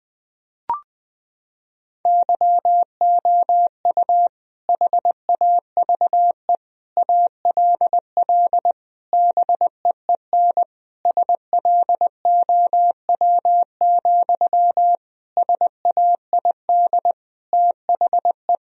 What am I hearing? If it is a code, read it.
Morse code, heard as a keyed tone, about 20 wpm, 700 Hz. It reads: YOU HAVE ALL BEEN SLOW, SAID THE
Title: Sentences from Top 1000 Words - Review Part 2 - 20wpm